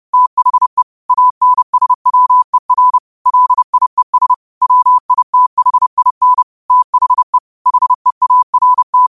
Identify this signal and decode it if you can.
Morse code, heard as a keyed tone, about 30 wpm, 1000 Hz. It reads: THE ANSWER LIES WITHIN THE HEART
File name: qotdfiles / 30wpm